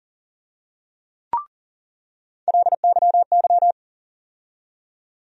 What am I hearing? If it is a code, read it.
Morse code, heard as a keyed tone, about 40 wpm, 700 Hz. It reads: LYY
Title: USA Call Sign Postfixes - 3 Letters 40wpm